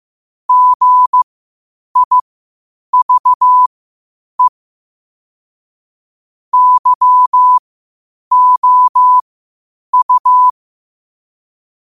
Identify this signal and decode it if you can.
Morse code, heard as a keyed tone, about 15 wpm, spaced as for 8 wpm, 1000 Hz. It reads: GIVE YOU
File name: qotdfiles / 8wpm